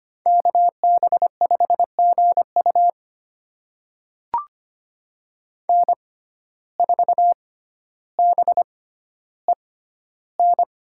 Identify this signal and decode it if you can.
Morse code, heard as a keyed tone, about 25 wpm, spaced as for 13 wpm, 700 Hz. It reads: KB5GU N4BEN
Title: USA Call Signs - 6x Character Spacing - 25wpm